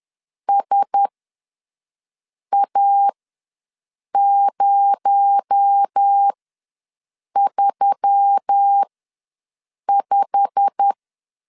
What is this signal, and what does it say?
Morse code, heard as a keyed tone, about 10 wpm, spaced as for 6 wpm, 800 Hz. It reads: SA035
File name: LEZIONI / 8 YJ45 / 8 YJ45 ws 10wpm 3843